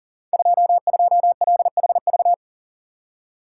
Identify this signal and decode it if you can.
Morse code, heard as a keyed tone, about 43 wpm, 690 Hz. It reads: 22LHV
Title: Code Groups - 5 Letter:Numbers - 40wpm